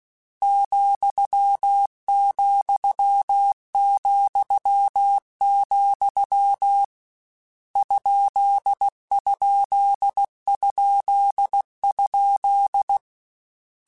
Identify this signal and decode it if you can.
Morse code, heard as a keyed tone, about 16 wpm, 800 Hz. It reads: ,,,, ????
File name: Misc / Misc-mp3s / Punct @15